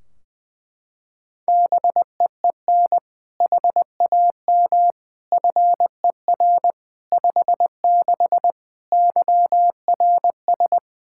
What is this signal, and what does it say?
Morse code, heard as a keyed tone, about 21 wpm, 700 Hz. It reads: BEEN HAM FER 56 YRS
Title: Practice QSO - Exchange 3 - More Info - 20wpm